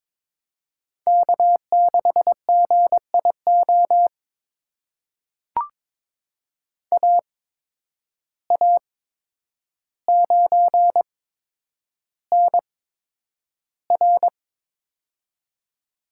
Morse code, heard as a keyed tone, about 22 wpm, spaced as for 9 wpm, 700 Hz. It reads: K6GIO AA9NR
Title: USA Call Signs - 5 Characters - 8x Character Spacing - 22wpm